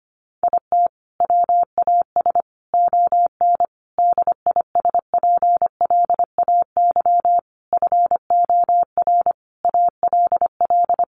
Morse code, heard as a keyed tone, about 25 wpm, 700 Hz. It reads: IT WAS ON DISPLAY FOR ALL
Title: Sentences from Top 1200 Words 25wpm